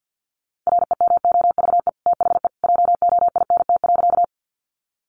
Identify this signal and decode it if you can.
Morse code, heard as a keyed tone, about 51 wpm, 700 Hz. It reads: REMOVE THE JOINT.
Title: Sentences from Top 1300 Words 50wpm